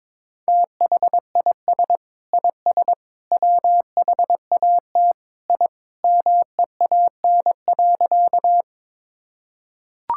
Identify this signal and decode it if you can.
Morse code, heard as a keyed tone, about 22 wpm, 700 Hz. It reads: THIS IS WHAT I MEAN.
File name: practice / 22wpm / Sentences with 3-Word Phrases - 22wpm